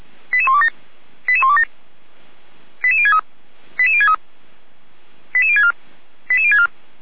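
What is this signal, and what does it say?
Five-tone selective call (ZVEI), heard as quick runs of five tones, tones of about 70 ms. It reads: 88127 88127 88063 88063 88064 88064
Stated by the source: ZVEI five-tone call 88128 received by SDR, logged on the Feuerwehr Orlamünde server.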